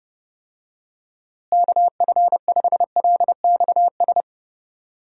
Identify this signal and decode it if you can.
Morse code, heard as a keyed tone, about 32 wpm, 700 Hz. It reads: KF5LXS
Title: USA Call Signs - 6 Characters - Repeated 2x - 30wpm